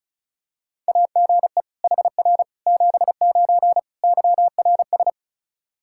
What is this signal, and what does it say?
Morse code, heard as a keyed tone, about 34 wpm, 700 Hz. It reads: AGE HR 79 YRS
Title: Practice QSO - Exchange 3 - More Info - 35wpm